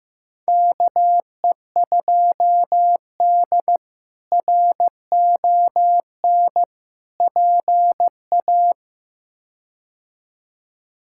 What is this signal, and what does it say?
Morse code, heard as a keyed tone, about 15 wpm, 700 Hz. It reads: KE2D RON PA
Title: K1USN SST QSO Practice - 15wpm